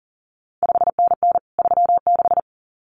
Morse code, heard as a keyed tone, about 38 wpm, 700 Hz. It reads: 5NN 36
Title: CQ WW DX Contest Exchange Encore - 40wpm